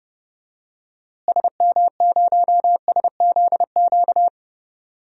Morse code, heard as a keyed tone, about 31 wpm, 700 Hz. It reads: SM0SZQ
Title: DXCC-100 Call Signs - 6 Characters - Repeated 3x - 30wpm